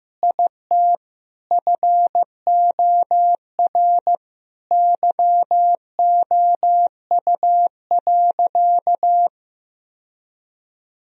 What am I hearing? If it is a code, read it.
Morse code, heard as a keyed tone, about 15 wpm, 700 Hz. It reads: IT FOR YOU.
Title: Sentences from Top 1000 Words - Review Part 4 - 15wpm